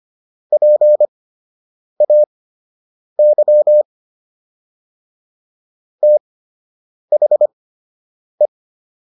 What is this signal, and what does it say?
Morse code, heard as a keyed tone, about 25 wpm, spaced as for 8 wpm, 600 Hz. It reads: PAY THE